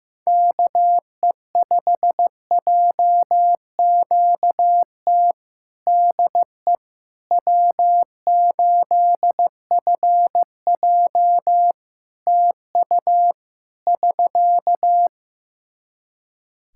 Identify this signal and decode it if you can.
Morse code, heard as a keyed tone, about 15 wpm, 700 Hz. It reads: KE5JQT DE W8FJ TU <SK>
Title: Practice QSO - Exchange 4 - Ending - 15wpm